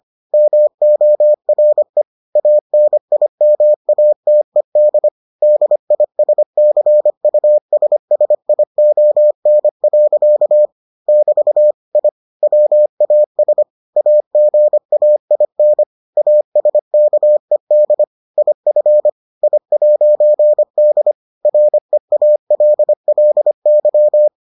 Morse code, heard as a keyed tone, about 25 wpm, 600 Hz. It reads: MORE ANIMATED DISCUSSION. <BT> I WAS AGAIN ASKED IF I'D REALLY